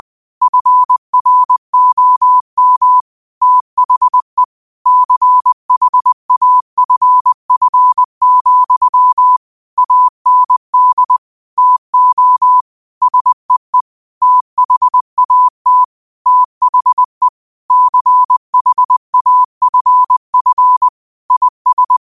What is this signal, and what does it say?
Morse code, heard as a keyed tone, about 20 wpm, 1000 Hz. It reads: FROM THE CHAFF, AND TO SEE THAT THE CHAFF IS
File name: qotdfiles / 20wpm